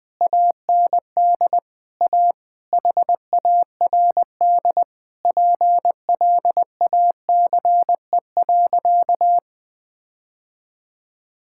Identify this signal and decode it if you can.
Morse code, heard as a keyed tone, about 21 wpm, 700 Hz. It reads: AND A HARD PLACE.
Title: Sentences with 8-Word Phrases - 20wpm